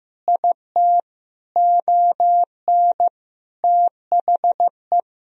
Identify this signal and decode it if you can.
Morse code, heard as a keyed tone, about 16 wpm, 700 Hz. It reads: IT ON THE
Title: Sentences from Top 1000 Words - Review Part 2 - 15wpm